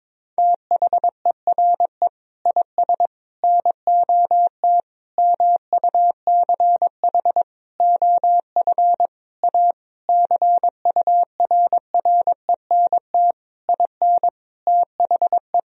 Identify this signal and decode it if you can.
Morse code, heard as a keyed tone, about 22 wpm, 700 Hz. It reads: THERE IS NOT MUCH OF A CURRENT IN THE
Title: Sentences from Top 1000 Words - Review Part 4 - 22wpm